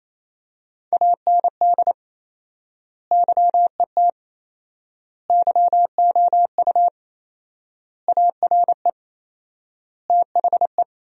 Morse code, heard as a keyed tone, about 28 wpm, 700 Hz. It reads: AND YET YOU ARE THE
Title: 4x Word-Spacing - Sentences from Top 500 Words - 28wpm